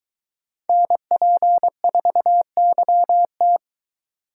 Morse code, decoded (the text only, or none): NP4YT